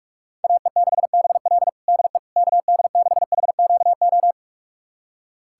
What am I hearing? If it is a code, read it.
Morse code, heard as a keyed tone, about 46 wpm, 690 Hz. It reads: AE6BL DE KD6HQO
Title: USA Callsign DE Callsign 45wpm